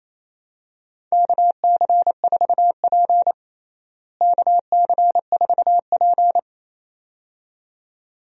Morse code, heard as a keyed tone, about 28 wpm, 700 Hz. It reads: KC4P KC4P
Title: USA Call Signs - 4 Characters - Repeated 2x - 28wpm